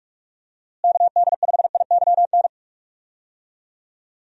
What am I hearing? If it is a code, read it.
Morse code, heard as a keyed tone, about 43 wpm, 700 Hz. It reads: KD5IYN
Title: USA Call Signs - 6 Characters - Repeated 3x - 45wpm